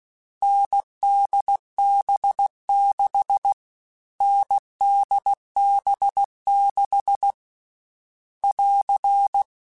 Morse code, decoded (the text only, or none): NDB6 NDB6 <AR>